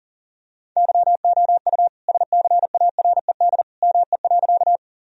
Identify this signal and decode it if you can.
Morse code, heard as a keyed tone, about 40 wpm, 700 Hz. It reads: YOU SCARED ME.
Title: Sentences from Top 1000 Words - Review Part 4 - 40wpm